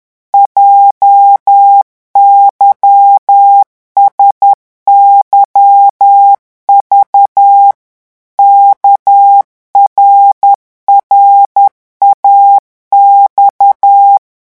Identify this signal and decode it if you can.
Morse code, heard as a keyed tone, about 10 wpm, 800 Hz. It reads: JYSYV KRRAX